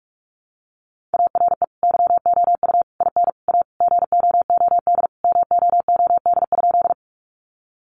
Morse code, heard as a keyed tone, about 44 wpm, 700 Hz. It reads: ARE YOU IN A GOOD MOOD?
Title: Sentences from Top 1100 Words 45wpm